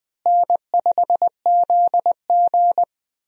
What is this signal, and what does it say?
Morse code, heard as a keyed tone, about 20 wpm, 700 Hz. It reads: N5ZG
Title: USA Call Signs - 4 Characters - Repeated 2x - 20wpm